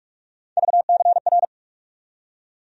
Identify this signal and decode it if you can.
Morse code, heard as a keyed tone, about 47 wpm, 700 Hz. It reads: VCR
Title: USA Call Sign Postfixes - 3 Letters 45wpm